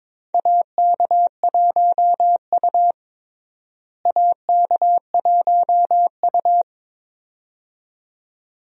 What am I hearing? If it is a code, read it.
Morse code, heard as a keyed tone, about 21 wpm, 700 Hz. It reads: AK1U AK1U
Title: USA Call Signs - 4 Characters - Repeated 2x - 22wpm